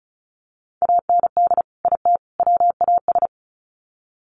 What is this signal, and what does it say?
Morse code, heard as a keyed tone, about 34 wpm, 700 Hz. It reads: AND IT WAS
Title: Sets of 3 Words 35wpm